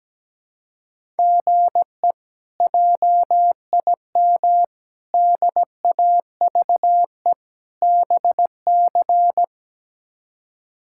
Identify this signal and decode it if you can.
Morse code, heard as a keyed tone, about 17 wpm, 700 Hz. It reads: GE JIM DAVE BC
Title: K1USN SST QSO Practice - 17wpm